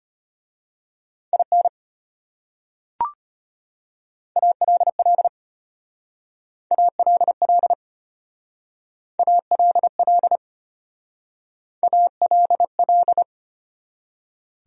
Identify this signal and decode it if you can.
Morse code, heard as a keyed tone, about 32 wpm, 700 Hz. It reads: IN ALL ALL ALL ALL